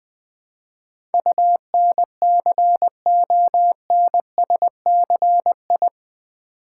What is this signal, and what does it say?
Morse code, heard as a keyed tone, about 20 wpm, 700 Hz. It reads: UNCONSCI
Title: Build A Word - 4 Letters At A Time - 11-Letter Words - 20wpm